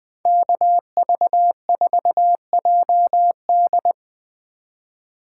Morse code, heard as a keyed tone, about 20 wpm, 700 Hz. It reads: KV4JD